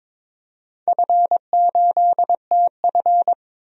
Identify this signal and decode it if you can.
Morse code, heard as a keyed tone, about 22 wpm, 700 Hz. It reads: F8TF
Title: DXCC-100 Call Signs - 4 Characters - 22wpm